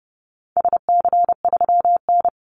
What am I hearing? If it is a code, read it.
Morse code, heard as a keyed tone, about 30 wpm, 700 Hz. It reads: SC3N
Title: Call Signs - 30wpm